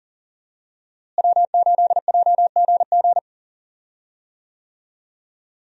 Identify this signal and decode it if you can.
Morse code, heard as a keyed tone, about 40 wpm, 700 Hz. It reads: W8JGG
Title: USA Call Signs - 5 Characters - Repeated 3x - 40wpm